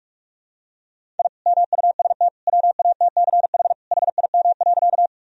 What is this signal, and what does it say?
Morse code, heard as a keyed tone, about 46 wpm, 700 Hz. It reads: I MUST WATCH HIM.
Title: Sentences from Top 1000 Words - Review Part 1 - 45wpm